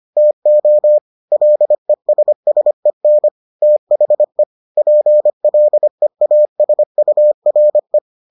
Morse code, heard as a keyed tone, about 25 wpm, 600 Hz. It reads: TO LESSEN THE PLEASURE